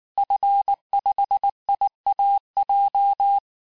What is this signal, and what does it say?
Morse code, heard as a keyed tone, about 19 wpm, 800 Hz. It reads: F5IAJ
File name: cours-du-lundi files / cours-1595--FIN-de-QSO-vitesse-18mpm